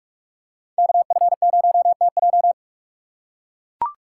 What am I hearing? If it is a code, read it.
Morse code, heard as a keyed tone, about 43 wpm, 690 Hz. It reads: KF0TJ
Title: USA Call Signs - 5 Characters - 4x Character Spacing - 45wpm